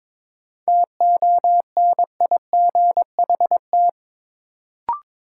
Morse code, heard as a keyed tone, about 22 wpm, 700 Hz. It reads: TONIGHT
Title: Sentences from Top 600 Words - New Word Drill - 22wpm